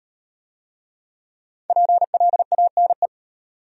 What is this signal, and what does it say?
Morse code, heard as a keyed tone, about 35 wpm, 700 Hz. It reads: PLANE